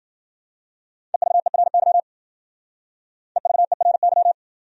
Morse code, heard as a keyed tone, about 51 wpm, 700 Hz. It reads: EVERY EVERY